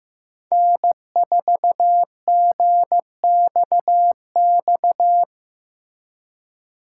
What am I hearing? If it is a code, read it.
Morse code, heard as a keyed tone, about 16 wpm, 700 Hz. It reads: N4GXX
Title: Speed Racing - USA 1x3 Call Signs 15wpm